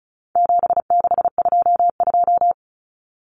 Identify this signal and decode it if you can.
Morse code, heard as a keyed tone, about 34 wpm, 700 Hz. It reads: 7622
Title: Common Cut Numbers - 4-Digit 35wpm